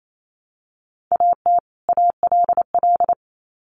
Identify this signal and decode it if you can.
Morse code, heard as a keyed tone, about 28 wpm, 700 Hz. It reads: AT ALL
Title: Sets of 2 Words 28wpm